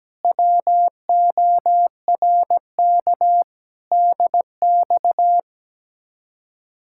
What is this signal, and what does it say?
Morse code, heard as a keyed tone, about 17 wpm, 700 Hz. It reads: WORK DX
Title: Speed-Racing Common QSO Phrases - 17wpm